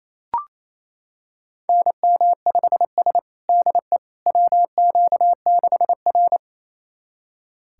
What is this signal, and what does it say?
Morse code, heard as a keyed tone, about 28 wpm, 700 Hz. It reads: NM5S DE WQ6R
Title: USA Callsign DE Callsign - 4 Characters - 28wpm